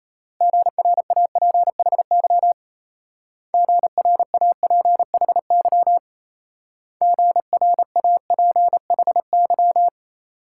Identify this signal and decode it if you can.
Morse code, heard as a keyed tone, about 36 wpm, 700 Hz. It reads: GRAPHY GRAPHY GRAPHY